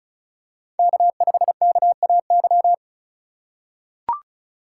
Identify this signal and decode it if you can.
Morse code, heard as a keyed tone, about 36 wpm, 700 Hz. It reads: K5KAY
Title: USA Call Signs - 5 Characters - 3x Character Spacing - 35wpm